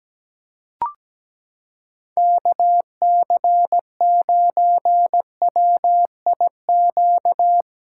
Morse code, heard as a keyed tone, about 17 wpm, 700 Hz. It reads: KC9WIQ
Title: USA Callsign DE Callsign - 6 Characters - 17wpm